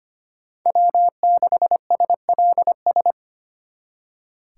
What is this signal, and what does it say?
Morse code, heard as a keyed tone, about 33 wpm, 700 Hz. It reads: W6SLS